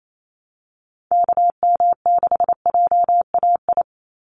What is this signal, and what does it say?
Morse code, heard as a keyed tone, about 28 wpm, 700 Hz. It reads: KM6JAI
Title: USA Call Signs 28wpm